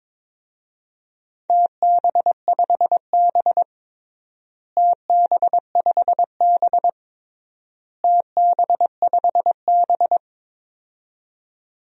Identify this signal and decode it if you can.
Morse code, heard as a keyed tone, about 21 wpm, 700 Hz. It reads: TB5B TB5B TB5B
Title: DXCC-100 Call Signs - 4 Characters - Repeated 3x - 22wpm